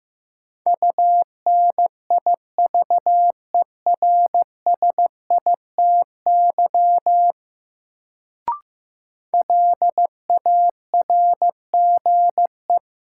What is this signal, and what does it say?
Morse code, heard as a keyed tone, about 15 wpm, 700 Hz. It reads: UNIVERSITY LARGE
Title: Sets of 4 Words - New Word Drill - 15wpm